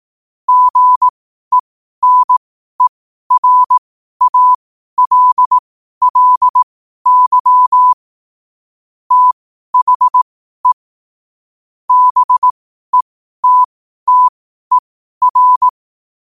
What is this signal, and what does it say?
Morse code, heard as a keyed tone, about 18 wpm, spaced as for 12 wpm, 1000 Hz. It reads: GENERALLY THE BETTER
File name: qotdfiles / 12wpm